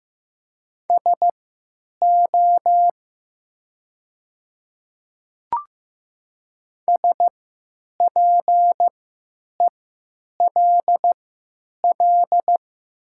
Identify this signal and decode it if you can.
Morse code, heard as a keyed tone, about 15 wpm, spaced as for 9 wpm, 700 Hz. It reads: SO SPELL
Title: 3x Character Spacing - Top 500 Words - 15wpm